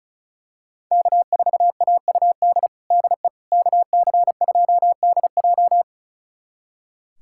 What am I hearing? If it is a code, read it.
Morse code, heard as a keyed tone, about 34 wpm, 700 Hz. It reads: K4AUD DE KC2DJ
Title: USA Callsign DE Callsign - 5 Characters - 35wpm